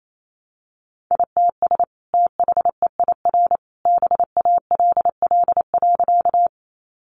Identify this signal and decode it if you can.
Morse code, heard as a keyed tone, about 28 wpm, 700 Hz. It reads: ITS THEIR BALL.
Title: Sentences from Top 900 Words 28wpm